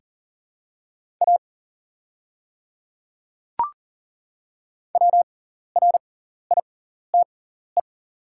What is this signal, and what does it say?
Morse code, heard as a keyed tone, about 40 wpm, spaced as for 14 wpm, 690 Hz. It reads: A WRITE
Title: 6x Character Spacing - Top 100 Words - 40wpm